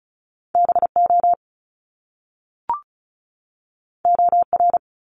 Morse code, heard as a keyed tone, about 36 wpm, 700 Hz. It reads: BO OR
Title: Top Two Letter Combinations in Words 35wpm